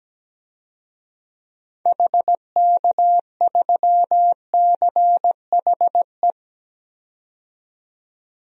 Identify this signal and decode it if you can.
Morse code, heard as a keyed tone, about 17 wpm, 700 Hz. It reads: HK3CHE
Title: DXCC-100 Call Signs - 6 Characters - 17wpm